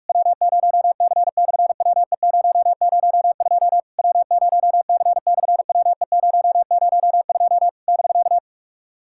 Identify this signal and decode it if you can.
Morse code, heard as a keyed tone, about 46 wpm, 700 Hz. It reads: W0C/WE002 W0C/WE002 <BK>
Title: SOTA QSO Practice 45wpm